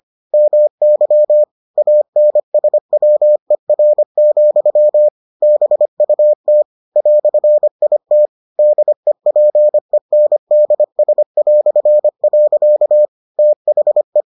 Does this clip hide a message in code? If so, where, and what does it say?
Yes: Morse code from 0.3 to 14.2 s, heard as a keyed tone, about 25 wpm, 600 Hz. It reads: MY ANSWER, BUT "IT DEPENDS". THE